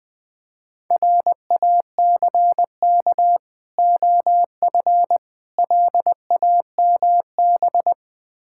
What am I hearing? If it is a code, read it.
Morse code, heard as a keyed tone, about 20 wpm, 690 Hz. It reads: RACK OF LAMB
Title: Difficult Words with Hints - 20wpm